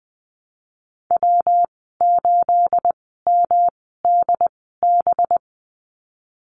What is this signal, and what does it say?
Morse code, heard as a keyed tone, about 20 wpm, spaced as for 15 wpm, 700 Hz. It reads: W8MDB
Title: 238 Lesson 38g - Letter q - Call Signs 20wpm